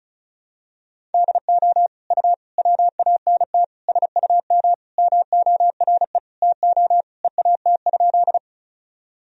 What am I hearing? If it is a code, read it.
Morse code, heard as a keyed tone, about 36 wpm, 700 Hz. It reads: DO U WANT SUM MORE TO EAT?
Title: Sentences from Top 300 Words and Abbreviations 35wpm